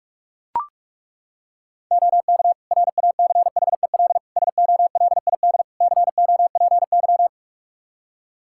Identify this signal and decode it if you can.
Morse code, heard as a keyed tone, about 43 wpm, 700 Hz. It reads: OK RACHEL SOLID COPY
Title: K1USN SST QSO Practice 45wpm